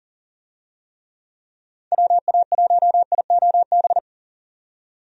Morse code, heard as a keyed tone, about 40 wpm, 700 Hz. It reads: WA1IOB
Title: USA Call Signs - 6 Characters - 40wpm